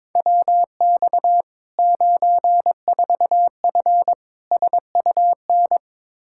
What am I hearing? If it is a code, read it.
Morse code, heard as a keyed tone, about 22 wpm, 700 Hz. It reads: WX 94F SUN
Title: Practice QSO - Exchange 2 - Info - 22wpm